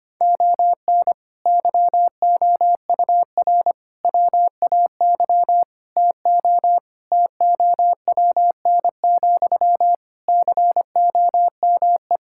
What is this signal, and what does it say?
Morse code, heard as a keyed tone, about 25 wpm, 700 Hz. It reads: ON YOUR WAY TO TOWN, COME
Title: Sentences from Top 1000 Words - Review Part 6 - 25wpm